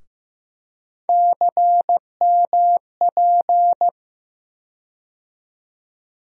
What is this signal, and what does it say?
Morse code, heard as a keyed tone, about 15 wpm, 700 Hz. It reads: CMP
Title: USA Call Sign Postfixes - 3 Letters 15wpm